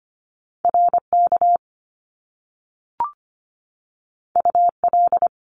Morse code, heard as a keyed tone, about 25 wpm, 700 Hz. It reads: RK UL
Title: Top Two Letter Combinations in Words 25wpm